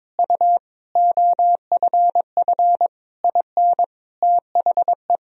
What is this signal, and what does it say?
Morse code, heard as a keyed tone, about 21 wpm, 700 Hz. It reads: U OFF IN THE
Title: Sentences from Top 200 Words and Abbreviations 22wpm